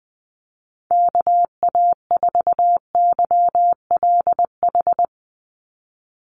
Morse code, heard as a keyed tone, about 20 wpm, 700 Hz. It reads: KA4YLH
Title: USA Call Signs 20wpm